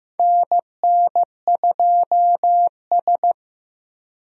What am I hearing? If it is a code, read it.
Morse code, heard as a keyed tone, about 15 wpm, 700 Hz. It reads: NN2S
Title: USA Call Signs - 4 Characters - Repeated 3x - 15wpm